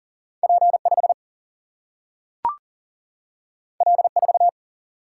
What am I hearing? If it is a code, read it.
Morse code, heard as a keyed tone, about 40 wpm, 700 Hz. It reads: P 5 L 4
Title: Single-Letter Single-Number 40wpm